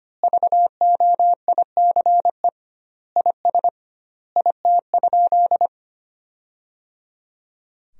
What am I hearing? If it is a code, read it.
Morse code, heard as a keyed tone, about 25 wpm, 700 Hz. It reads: VOICE IS IT?
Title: 2x Word-Spacing - Sentences from Top 500 Words - 25wpm